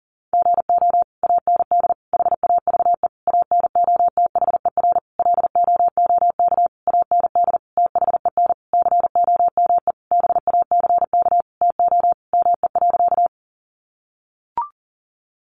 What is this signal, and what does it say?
Morse code, heard as a keyed tone, about 40 wpm, 700 Hz. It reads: GO AND HAVE ANOTHER LOOK AND THEN COME BACK TO ME.
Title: Sentences from Top 600 Words 40wpm